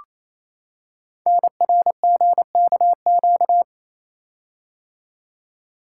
Morse code, heard as a keyed tone, about 27 wpm, 700 Hz. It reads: NRGKQ
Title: Code Groups - 5 Letters - 28wpm